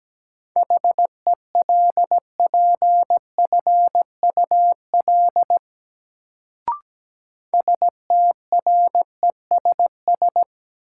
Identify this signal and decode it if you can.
Morse code, heard as a keyed tone, about 17 wpm, 700 Hz. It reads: HELPFUL STRESS
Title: Sentences from Top 1000 Words - Review Part 6 - New Word Drill - 17wpm